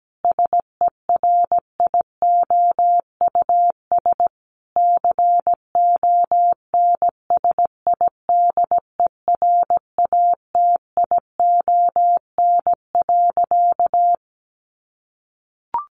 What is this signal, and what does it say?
Morse code, heard as a keyed tone, about 17 wpm, 700 Hz. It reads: SERIOUS CONSIDERATION.
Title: Sentences from Top 1500 Words 17wpm